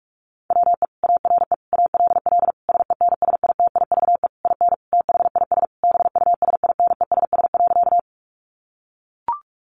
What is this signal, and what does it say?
Morse code, heard as a keyed tone, about 44 wpm, 700 Hz. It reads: WE ARE ALL SENSITIVE IN THIS BUSINESS.
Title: Sentences from Top 1400 Words 45wpm